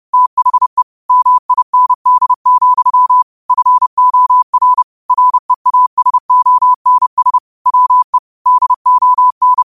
Morse code, heard as a keyed tone, about 30 wpm, 1000 Hz. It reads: THE MIND, FOR REASONS WE DON